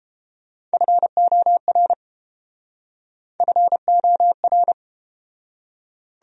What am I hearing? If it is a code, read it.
Morse code, heard as a keyed tone, about 30 wpm, 700 Hz. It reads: FOR FOR